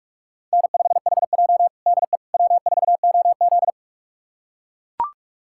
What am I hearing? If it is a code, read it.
Morse code, heard as a keyed tone, about 42 wpm, 700 Hz. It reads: N5HJ DE W4OZ
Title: USA Callsign DE Callsign - 4 Characters - 45wpm